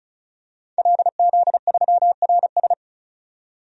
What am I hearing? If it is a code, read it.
Morse code, heard as a keyed tone, about 34 wpm, 700 Hz. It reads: LZ3RS
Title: DXCC-100 Call Signs - 5 Characters - 35wpm